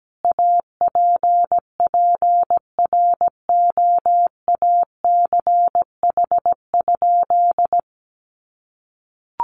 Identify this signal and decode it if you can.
Morse code, heard as a keyed tone, about 17 wpm, 700 Hz. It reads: APPROACH?
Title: Sentences from Top 1500 Words 17wpm